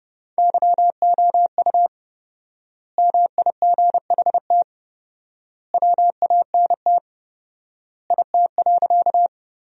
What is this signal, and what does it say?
Morse code, heard as a keyed tone, about 30 wpm, 700 Hz. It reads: YOU MIGHT WANT IT.
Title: 4x Word-Spacing - Sentences from Top 300 Words - 30wpm